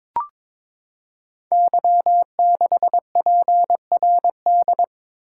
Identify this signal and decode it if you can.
Morse code, heard as a keyed tone, about 22 wpm, 700 Hz. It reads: Y6PRD
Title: Code Groups - 5 Letter:Numbers - 22wpm